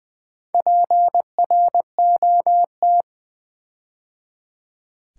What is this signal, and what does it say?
Morse code, heard as a keyed tone, about 20 wpm, 700 Hz. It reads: PROT
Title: Build A Word - 4 Letters At A Time - 10-Letter Words - 20wpm